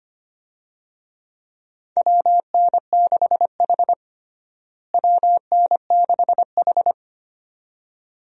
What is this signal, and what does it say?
Morse code, heard as a keyed tone, about 25 wpm, 700 Hz. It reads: WN6H WN6H
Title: USA Call Signs - 4 Characters - Repeated 2x - 25wpm